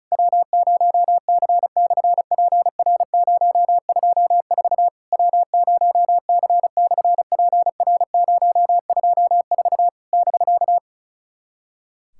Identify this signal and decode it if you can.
Morse code, heard as a keyed tone, about 36 wpm, 700 Hz. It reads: W0C/PR024 W0C/PR024 <BK>
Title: SOTA QSO Practice 35wpm